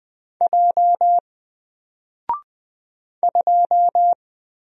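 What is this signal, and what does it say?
Morse code, heard as a keyed tone, about 20 wpm, 700 Hz. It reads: J 2
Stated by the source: Single letter and number Morse drill at 20 wpm, from N4LCD.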